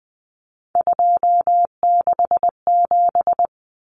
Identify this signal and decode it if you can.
Morse code, heard as a keyed tone, about 20 wpm, 700 Hz. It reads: 267
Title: Common Cut Numbers - 3-Digit 20wpm